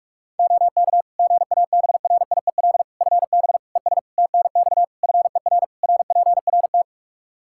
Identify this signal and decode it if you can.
Morse code, heard as a keyed tone, about 46 wpm, 700 Hz. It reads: OK GABRIEL FB ES TNX FER RPRT
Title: Practice QSO - Exchange 2 - Info - 45wpm